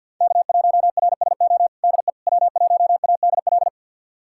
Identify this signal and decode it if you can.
Morse code, heard as a keyed tone, about 46 wpm, 700 Hz. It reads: K1RSO DE W1ADL
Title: USA Callsign DE Callsign - 5 Characters - 50wpm